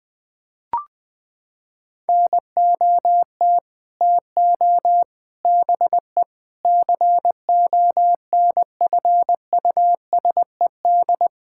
Morse code, heard as a keyed tone, about 20 wpm, 700 Hz. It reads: NOT TO BE CONFUSED
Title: Sets of 6 Words - 20wpm